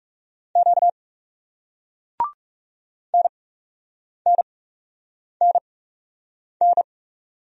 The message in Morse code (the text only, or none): Q NNNN